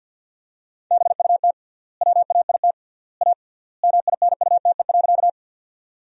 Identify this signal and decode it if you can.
Morse code, heard as a keyed tone, about 50 wpm, 700 Hz. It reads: BUT WAIT A MINUTE.
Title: 3x Word-Spacing - Sentences from Top 500 Words - 50wpm